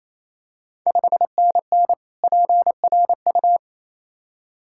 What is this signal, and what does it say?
Morse code, heard as a keyed tone, about 28 wpm, 700 Hz. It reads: 5NN PRU
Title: Ontario QSO Party Exchange 28wpm